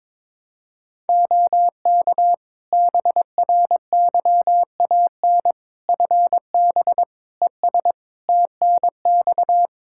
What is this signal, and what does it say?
Morse code, heard as a keyed tone, about 21 wpm, 700 Hz. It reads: OK BRYAN FB ES TNX
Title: Practice QSO - Exchange 2 - Info - 22wpm